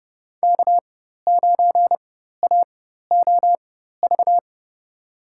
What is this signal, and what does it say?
Morse code, heard as a keyed tone, about 32 wpm, spaced as for 14 wpm, 700 Hz. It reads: K9AOV
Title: USA Call Signs - 4x Character Spacing - 30wpm